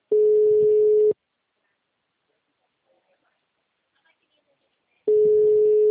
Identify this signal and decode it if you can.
Ringback tone (Europe, 425 Hz).